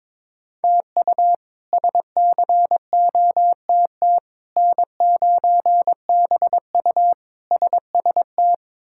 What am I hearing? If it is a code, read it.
Morse code, heard as a keyed tone, about 21 wpm, 700 Hz. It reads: TU SCOTT N9BU SST
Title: K1USN SST QSO Practice - 22wpm